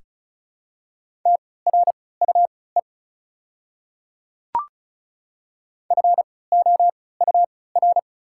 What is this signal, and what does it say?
Morse code, heard as a keyed tone, about 36 wpm, spaced as for 20 wpm, 690 Hz. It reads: TRUE FOUR
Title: 3x Character Spacing - Top 500 Words - 35wpm